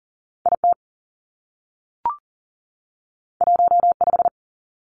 Morse code, heard as a keyed tone, about 40 wpm, 700 Hz. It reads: I T 1 5
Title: Two Challenging Letter-Number Combinations 40wpm